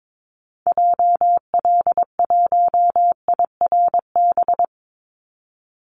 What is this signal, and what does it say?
Morse code, heard as a keyed tone, about 22 wpm, 700 Hz. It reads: JL1IRB
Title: Call Signs Repeated 3x - 22wpm